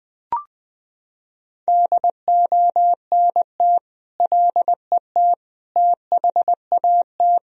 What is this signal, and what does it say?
Morse code, heard as a keyed tone, about 21 wpm, 700 Hz. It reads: DONT LET THAT
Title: Sentences from Top 1000 Words - Review Part 1 - 20wpm